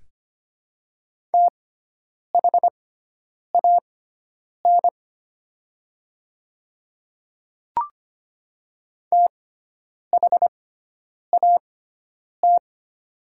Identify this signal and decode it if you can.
Morse code, heard as a keyed tone, about 25 wpm, spaced as for 9 wpm, 700 Hz. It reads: THAN THAT
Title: 6x Character Spacing - Top 300 Words - 25wpm